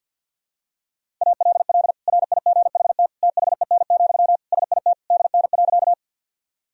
Morse code, heard as a keyed tone, about 52 wpm, 700 Hz. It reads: ALL RIGHT THEN, SIT DN.
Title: Sentences from Top 500 Words and Abbreviations 50wpm